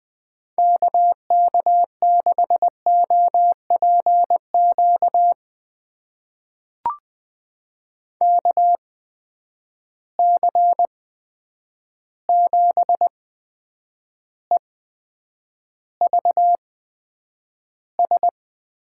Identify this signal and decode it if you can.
Morse code, heard as a keyed tone, about 20 wpm, spaced as for 9 wpm, 700 Hz. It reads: KK6OPQ KC7EVS
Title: USA Call Signs - 6 Characters - 8x Character Spacing - 20wpm